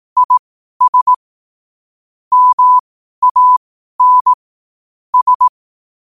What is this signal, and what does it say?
Morse code, heard as a keyed tone, about 18 wpm, spaced as for 12 wpm, 1000 Hz. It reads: IS MAN S
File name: qotdfiles / 12wpm